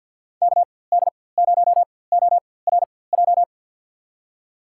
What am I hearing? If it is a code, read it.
Morse code, heard as a keyed tone, about 48 wpm, spaced as for 23 wpm, 700 Hz. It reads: KD0ORJ
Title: USA Call Signs - 6 Characters - 4x Character Spacing - 50wpm